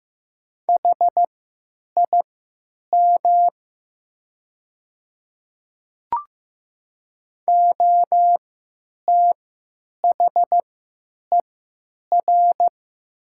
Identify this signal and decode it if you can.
Morse code, heard as a keyed tone, about 15 wpm, spaced as for 9 wpm, 700 Hz. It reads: HIM OTHER
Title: 3x Character Spacing - Top 200 Words - 15wpm